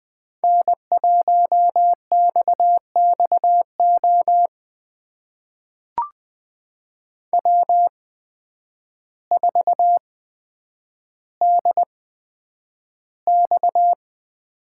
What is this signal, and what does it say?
Morse code, heard as a keyed tone, about 20 wpm, spaced as for 9 wpm, 700 Hz. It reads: N1XXO W4DX
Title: USA Call Signs - 5 Characters - 8x Character Spacing - 20wpm